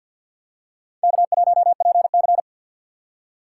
Morse code, heard as a keyed tone, about 53 wpm, 700 Hz. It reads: K1PC
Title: USA Call Signs - 4x Character Spacing - 50wpm